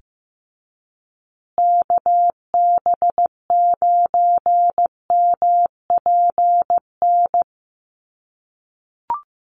Morse code, heard as a keyed tone, about 15 wpm, 700 Hz. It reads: KB9MPN
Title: USA Call Signs Repeated 3x 15wpm